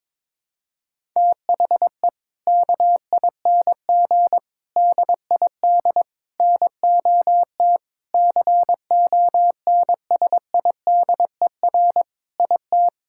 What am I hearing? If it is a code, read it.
Morse code, heard as a keyed tone, about 22 wpm, 700 Hz. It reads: THE KING DID NOT CONSIDER IT
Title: Sentences from Top 1000 Words - Review Part 5 - 22wpm